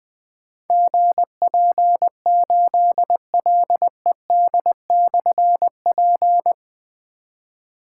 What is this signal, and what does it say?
Morse code, heard as a keyed tone, about 20 wpm, 700 Hz. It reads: GP8LED/P
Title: Difficult Call Signs 20wpm